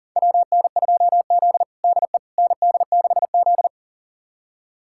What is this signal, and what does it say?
Morse code, heard as a keyed tone, about 40 wpm, 700 Hz. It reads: WN2Z DE ND6Z
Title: USA Callsign DE Callsign - 4 Characters - 40wpm